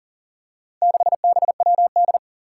Morse code, heard as a keyed tone, about 40 wpm, 700 Hz. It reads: 6BWD
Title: Four Challenging Letter-Number Combinations 40wpm